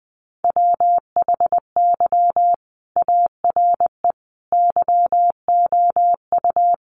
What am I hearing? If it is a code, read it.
Morse code, heard as a keyed tone, about 20 wpm, 700 Hz. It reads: WHY ARE YOU
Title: Sentences from Top 200 Words 20wpm